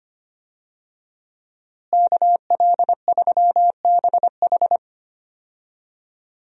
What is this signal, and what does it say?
Morse code, heard as a keyed tone, about 25 wpm, 700 Hz. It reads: KL3BH
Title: USA Call Signs - 5 Characters - Repeated 3x - 25wpm